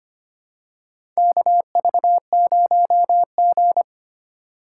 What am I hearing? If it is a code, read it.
Morse code, heard as a keyed tone, about 25 wpm, 700 Hz. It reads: KV0G
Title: USA Call Signs - 4x Character Spacing - 25wpm